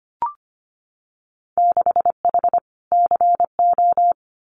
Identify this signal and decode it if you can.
Morse code, heard as a keyed tone, about 25 wpm, 700 Hz. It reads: 6H CO
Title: Winter Field Day Exchanges Encore 25wpm